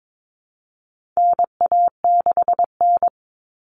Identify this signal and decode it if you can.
Morse code, heard as a keyed tone, about 22 wpm, 700 Hz. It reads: NA6N
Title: Common Cut Numbers - 4-Digit 22wpm